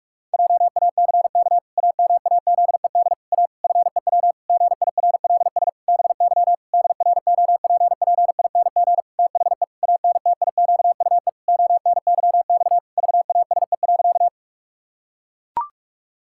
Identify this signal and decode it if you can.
Morse code, heard as a keyed tone, about 41 wpm, 700 Hz. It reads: JACK AMAZED A FEW GIRLS BY DROPPING THE ANTIQUE ONYX VASE.
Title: Pangram Challenge v2 - 45wpm